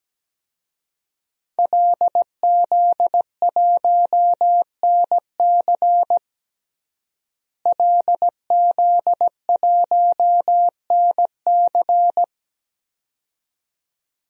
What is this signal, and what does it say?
Morse code, heard as a keyed tone, about 17 wpm, 700 Hz. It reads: LZ1NC LZ1NC
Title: DXCC-100 Call Signs - 5 Characters - Repeated 2x - 17wpm